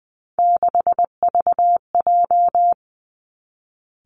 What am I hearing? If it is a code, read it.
Morse code, heard as a keyed tone, about 20 wpm, 700 Hz. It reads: 6VJ
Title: Three Challenging Letter-Number Combinations 20wpm